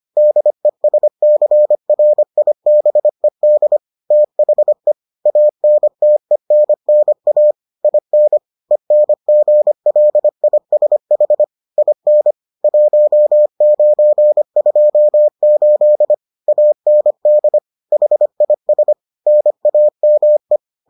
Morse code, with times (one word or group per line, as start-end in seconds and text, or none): 0.2-3.8 DESCRIBED
4.1-4.9 THE
5.3-7.5 ANTENNA
7.8-8.4 IN
8.7-11.4 ENGLISH
11.8-12.3 IN
12.6-16.1 1928
16.5-17.6 AND
17.9-18.9 HIS
19.3-20.6 NAME